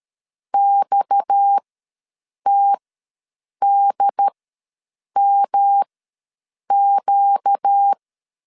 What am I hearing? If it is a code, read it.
Morse code, heard as a keyed tone, about 13 wpm, spaced as for 7 wpm, 800 Hz. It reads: XTDMQ
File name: LEZIONI / 6 PKX0 / 6 PKX ws 12wpm 3106